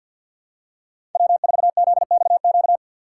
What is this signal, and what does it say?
Morse code, heard as a keyed tone, about 50 wpm, 690 Hz. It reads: W4ZXQ
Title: USA Call Signs - 5 Characters - 3x Character Spacing - 50wpm